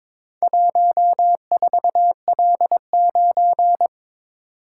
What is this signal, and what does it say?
Morse code, heard as a keyed tone, about 22 wpm, 700 Hz. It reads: 14L9
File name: practice / 22wpm / Four Letter-Number Combinations - 22wpm